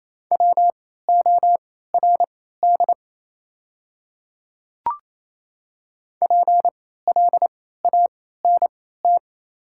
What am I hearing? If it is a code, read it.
Morse code, heard as a keyed tone, about 28 wpm, spaced as for 16 wpm, 700 Hz. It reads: WORD PLANT
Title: 3x Character Spacing - Top 300 Words - 28wpm